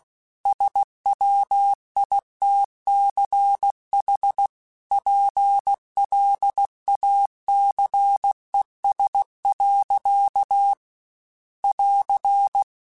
Morse code, heard as a keyed tone, about 16 wpm, 800 Hz. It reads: SWITCH PLACES. <AR>
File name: Chuckles / Chuckles-mp3s / MiddleAge @15